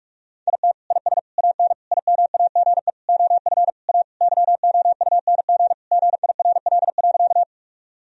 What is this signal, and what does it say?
Morse code, heard as a keyed tone, about 46 wpm, 700 Hz. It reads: IT IS AN IMAGE OF A YOUNG GIRL.
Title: Sentences from Top 1000 Words - Review Part 4 - 45wpm